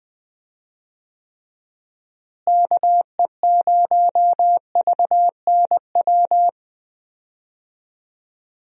Morse code, heard as a keyed tone, about 21 wpm, 700 Hz. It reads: KE0VNW
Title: USA Call Signs - 6 Characters - Repeated 3x - 20wpm